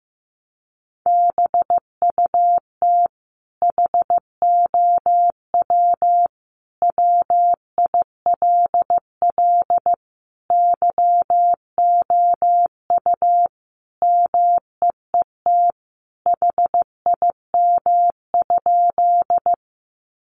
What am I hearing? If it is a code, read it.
Morse code, heard as a keyed tone, about 15 wpm, 700 Hz. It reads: BUT HOW WILL YOU MEET HIM?
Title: Sentences from Top 1100 Words 15wpm